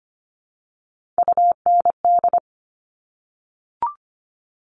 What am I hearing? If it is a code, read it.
Morse code, heard as a keyed tone, about 25 wpm, 700 Hz. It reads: UND
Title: Top Three Letter Combinations in Words 25wpm